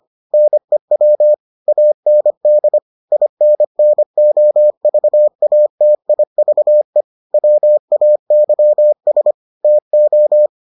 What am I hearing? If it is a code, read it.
Morse code, heard as a keyed tone, about 25 wpm, 600 Hz. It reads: NEW AND INNOVATIVE WAYS TO